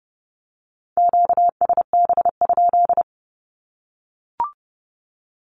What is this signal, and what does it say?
Morse code, heard as a keyed tone, about 30 wpm, 700 Hz. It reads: QSB?
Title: Q-Codes 30wpm